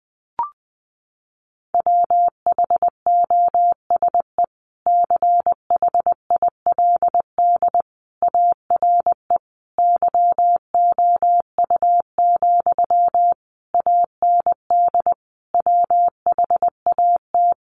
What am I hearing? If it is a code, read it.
Morse code, heard as a keyed tone, about 20 wpm, 700 Hz. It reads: WHOSE CHILD ARE YOU, AND WHAT
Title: Sentences from Top 1200 Words 20wpm